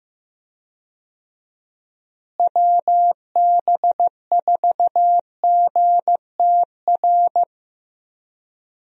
Morse code, heard as a keyed tone, about 15 wpm, 700 Hz. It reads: WB4GTR
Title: USA Call Signs - 6 Characters - Repeated 3x - 15wpm